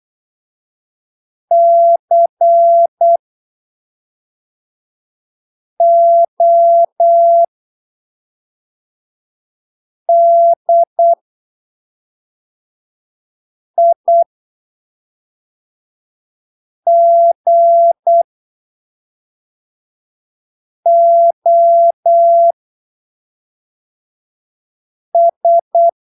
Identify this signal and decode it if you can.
Morse code, heard as a keyed tone, about 8 wpm, spaced as for 3 wpm, 680 Hz. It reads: CODIGOS